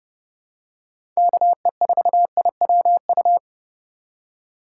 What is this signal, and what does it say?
Morse code, heard as a keyed tone, about 32 wpm, 700 Hz. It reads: KE4IWU
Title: USA Call Signs - 4x Character Spacing - 30wpm